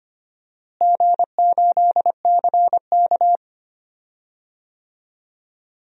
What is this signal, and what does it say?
Morse code, heard as a keyed tone, about 25 wpm, 700 Hz. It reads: G8CK
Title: DXCC-100 Call Signs - 4 Characters - Repeated 3x - 25wpm